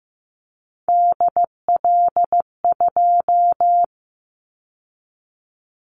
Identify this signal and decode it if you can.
Morse code, heard as a keyed tone, about 15 wpm, 700 Hz. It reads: DL2
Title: Callsign Prefixes for Top 25 DXCC - 15wpm